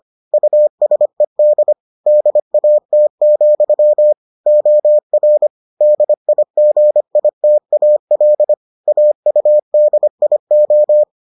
Morse code, heard as a keyed tone, about 25 wpm, 600 Hz. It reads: USED DAT, OR DIGITAL AUDIO